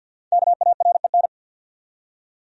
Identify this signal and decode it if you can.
Morse code, heard as a keyed tone, about 51 wpm, 700 Hz. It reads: KAREN
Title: USA Common Names - 50wpm